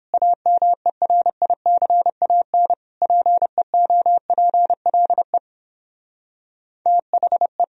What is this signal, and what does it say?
Morse code, heard as a keyed tone, about 30 wpm, 700 Hz. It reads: AMERICAN PEOPLE THE